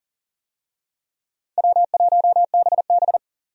Morse code, heard as a keyed tone, about 37 wpm, 700 Hz. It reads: W1BB
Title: USA Call Signs - 4 Characters - 40wpm